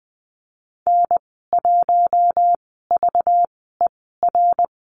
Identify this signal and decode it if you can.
Morse code, heard as a keyed tone, about 20 wpm, spaced as for 15 wpm, 700 Hz. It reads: N1VER
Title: 219 Lesson 35g - Number 0 - Call Signs 20wpm